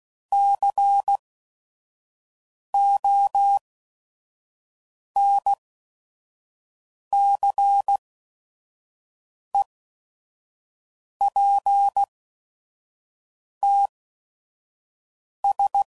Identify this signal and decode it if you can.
Morse code, heard as a keyed tone, about 16 wpm, spaced as for 5 wpm, 800 Hz. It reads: CONCEPTS
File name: CW Code Groups @5WPM / Technical Subjects / Gates Intro